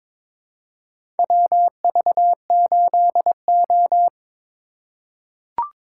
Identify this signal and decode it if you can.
Morse code, heard as a keyed tone, about 22 wpm, 700 Hz. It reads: WV8O
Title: USA Call Signs - 4 Characters - 2x Character Spacing - 22wpm